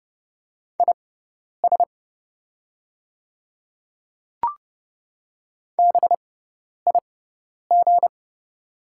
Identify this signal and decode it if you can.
Morse code, heard as a keyed tone, about 30 wpm, spaced as for 10 wpm, 690 Hz. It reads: IS BIG
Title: 6x Character Spacing - Top 200 Words - 30wpm